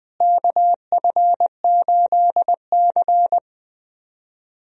Morse code, heard as a keyed tone, about 20 wpm, 700 Hz. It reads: KF8C
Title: USA Call Signs - 4 Characters - Repeated 3x - 20wpm